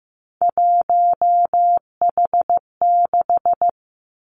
Morse code, heard as a keyed tone, about 15 wpm, 700 Hz. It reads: 1H6
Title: Three Challenging Letter-Number Combinations 15wpm